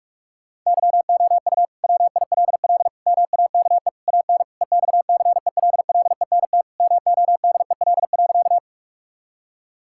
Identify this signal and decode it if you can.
Morse code, heard as a keyed tone, about 46 wpm, 700 Hz. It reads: YOU WILL MAKE AN EXCELLENT MODEL.
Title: Sentences from Top 1000 Words - Review Part 5 - 45wpm